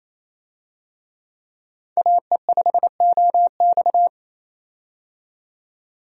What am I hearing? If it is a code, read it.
Morse code, heard as a keyed tone, about 28 wpm, 700 Hz. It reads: AE5OX
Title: USA Call Signs - 5 Characters - 28wpm